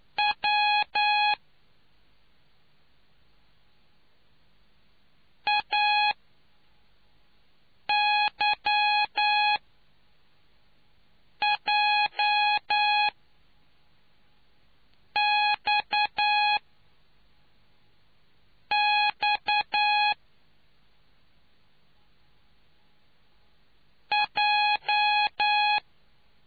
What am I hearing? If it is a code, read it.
Morse code, heard as a keyed tone, about 9 wpm, 830 Hz. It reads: W A Y J X X J